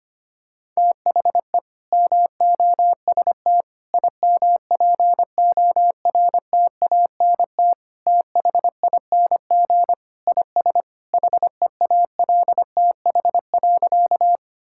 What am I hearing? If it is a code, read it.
Morse code, heard as a keyed tone, about 25 wpm, 700 Hz. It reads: THE MOST IMPORTANT THING IS HEALTH.
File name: practice / 25wpm / Sentences with 3-Word Phrases - 25wpm